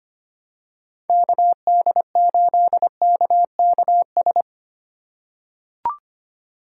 Morse code, heard as a keyed tone, about 25 wpm, 700 Hz. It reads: KD8KKS
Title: USA Call Signs - 6 Characters - 2x Character Spacing - 25wpm